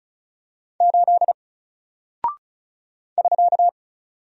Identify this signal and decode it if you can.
Morse code, heard as a keyed tone, about 36 wpm, 700 Hz. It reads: 8 <SK>
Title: Single Character (Letter:Number:Punctuation:Prosign) Alt 35wpm